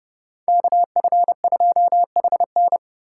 Morse code, heard as a keyed tone, about 39 wpm, 700 Hz. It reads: KF2HN